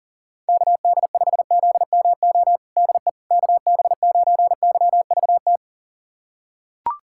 Morse code, heard as a keyed tone, about 40 wpm, 700 Hz. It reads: KD5ZMO DE KB9YVT
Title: USA Callsign DE Callsign - 6 Characters - 40wpm